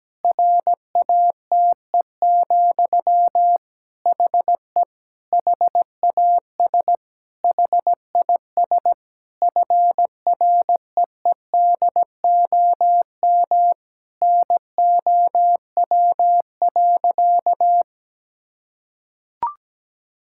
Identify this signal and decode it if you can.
Morse code, heard as a keyed tone, about 17 wpm, 700 Hz. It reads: RATE, HE HAS HIS FREEDOM NOW.
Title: Sentences from Top 1000 Words - Review Part 5 - 17wpm